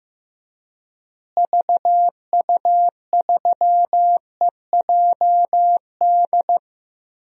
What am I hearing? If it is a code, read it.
Morse code, heard as a keyed tone, about 15 wpm, 700 Hz. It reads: VU3EJD
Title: DXCC-100 Call Signs - 6 Characters - Repeated 3x - 15wpm